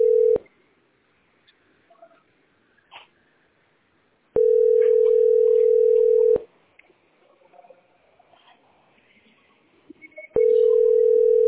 Ringback tone (North America), 440 + 480 Hz.